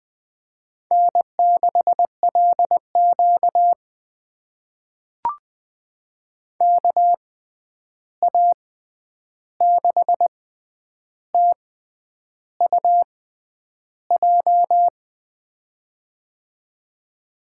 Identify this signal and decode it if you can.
Morse code, heard as a keyed tone, about 20 wpm, spaced as for 9 wpm, 700 Hz. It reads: N6LQ KA6TUJ
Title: USA Call Signs - 6x Character Spacing - 20wpm